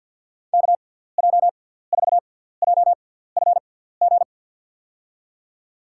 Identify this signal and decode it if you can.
Morse code, heard as a keyed tone, about 52 wpm, spaced as for 17 wpm, 700 Hz. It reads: KJ4JFG
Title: USA Call Signs - 6 Characters - 6x Character Spacing - 50wpm